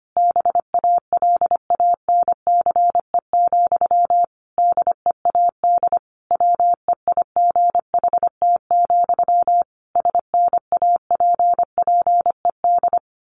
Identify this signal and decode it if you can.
Morse code, heard as a keyed tone, about 25 wpm, 700 Hz. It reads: BALANCE, DEAD WEIGHT, SNAPPED
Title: A Sound Of Thunder 25wpm